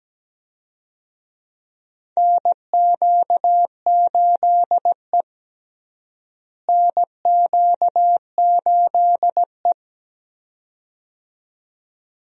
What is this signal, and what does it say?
Morse code, heard as a keyed tone, about 17 wpm, 700 Hz. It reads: NQ8E NQ8E
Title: USA Call Signs - 4 Characters - Repeated 2x - 17wpm